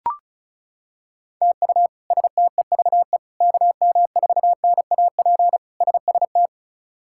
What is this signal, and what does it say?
Morse code, heard as a keyed tone, about 36 wpm, 700 Hz. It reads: TU STEVE KM4NAP SST
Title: K1USN SST QSO Practice - 35wpm